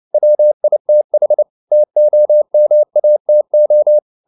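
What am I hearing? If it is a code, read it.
Morse code, heard as a keyed tone, about 29 wpm, 600 Hz. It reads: WITH TOMATO
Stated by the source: Morse code at 29 wpm, sending Ken Follett's novel Never in English, from CW med Gnister.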